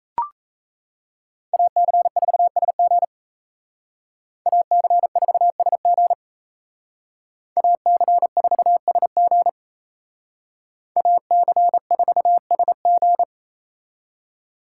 Morse code, heard as a keyed tone, about 32 wpm, 700 Hz. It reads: AC4SG AC4SG AC4SG AC4SG
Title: Speed Racing - USA 2x2 Call Signs 28wpm